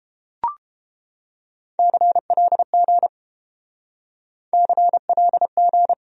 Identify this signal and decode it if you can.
Morse code, heard as a keyed tone, about 29 wpm, 700 Hz. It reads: CLG CLG